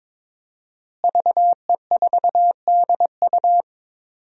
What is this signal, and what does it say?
Morse code, heard as a keyed tone, about 22 wpm, 700 Hz. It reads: VE4DU
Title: DXCC-100 Call Signs - 5 Characters - Repeated 2x - 22wpm